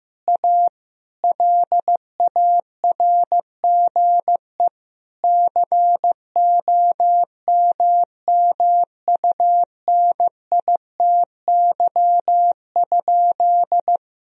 Morse code, heard as a keyed tone, about 15 wpm, 700 Hz. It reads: A LARGE COMMUNITY?
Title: Sentences from Top 1000 Words - Review Part 4 - 15wpm